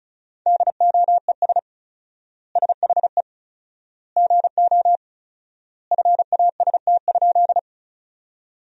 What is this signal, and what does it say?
Morse code, heard as a keyed tone, about 34 wpm, 700 Hz. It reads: DOES SHE GO FAST?
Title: 4x Word-Spacing - Sentences from Top 400 Words - 35wpm